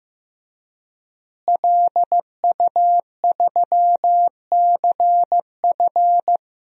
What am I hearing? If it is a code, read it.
Morse code, heard as a keyed tone, about 15 wpm, 700 Hz. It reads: LU3CF
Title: DXCC-100 Call Signs - 5 Characters - Repeated 3x - 15wpm